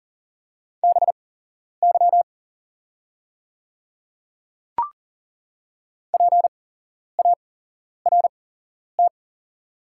Morse code, heard as a keyed tone, about 40 wpm, spaced as for 11 wpm, 700 Hz. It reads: BY PART
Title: 8x Character Spacing - Top 300 Words - 40wpm